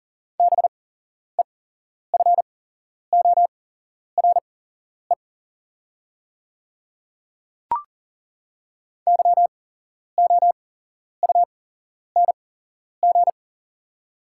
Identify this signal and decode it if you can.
Morse code, heard as a keyed tone, about 40 wpm, spaced as for 11 wpm, 690 Hz. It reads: BEFORE YOUNG
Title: 8x Character Spacing - Top 500 Words - 40wpm